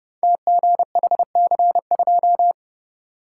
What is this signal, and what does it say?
Morse code, heard as a keyed tone, about 29 wpm, 690 Hz. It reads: TGHC2
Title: Code Groups - 5 Letter:Numbers - 30wpm